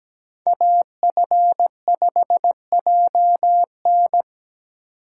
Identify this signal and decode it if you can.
Morse code, heard as a keyed tone, about 17 wpm, 700 Hz. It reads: AF5JN
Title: USA Call Signs - 5 Characters - 8x Character Spacing - 17wpm